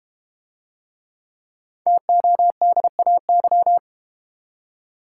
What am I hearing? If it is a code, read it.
Morse code, heard as a keyed tone, about 32 wpm, 700 Hz. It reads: TODAY